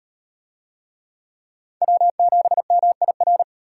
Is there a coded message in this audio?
Yes: Morse code from 1.8 to 3.4 s, heard as a keyed tone, about 35 wpm, 700 Hz. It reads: W7MIR